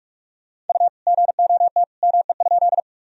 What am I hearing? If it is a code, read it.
Morse code, heard as a keyed tone, about 46 wpm, 700 Hz. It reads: U GOT ME?
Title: Sentences from Top 300 Words and Abbreviations 45wpm